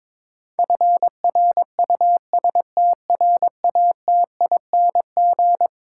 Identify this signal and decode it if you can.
Morse code, heard as a keyed tone, about 22 wpm, 700 Hz. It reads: FRUSTRATING
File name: practice / 22wpm / Eleven Letter Words - 22wpm